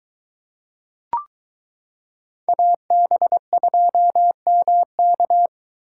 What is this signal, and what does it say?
Morse code, heard as a keyed tone, about 22 wpm, 700 Hz. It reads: AB2MK